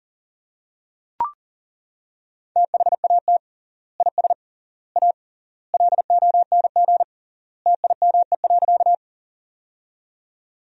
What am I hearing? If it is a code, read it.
Morse code, heard as a keyed tone, about 40 wpm, 700 Hz. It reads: THAT IS A LONG TIME.
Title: 3x Word-Spacing - Sentences from Top 100 Words - 40wpm